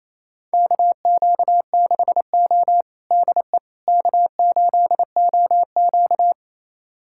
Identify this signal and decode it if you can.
Morse code, heard as a keyed tone, about 28 wpm, 700 Hz. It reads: KQ6O DE K8OQ
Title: USA Callsign DE Callsign - 4 Characters - 28wpm